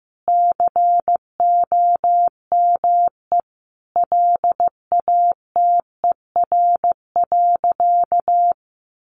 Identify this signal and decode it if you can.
Morse code, heard as a keyed tone, about 15 wpm, 700 Hz. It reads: COME LATER.
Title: Sentences from Top 700 Words 15wpm